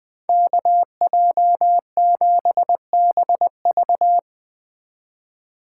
Morse code, heard as a keyed tone, about 20 wpm, 700 Hz. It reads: KJ7BV